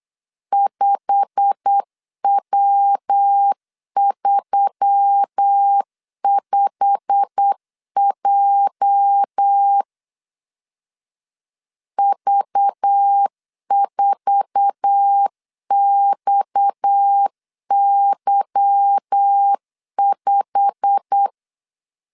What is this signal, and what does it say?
Morse code, heard as a keyed tone, about 8 wpm, 800 Hz. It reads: 5W35J V4XY5
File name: LEZIONI / 8 YJ45 / 8 YJ45 8wpm 3214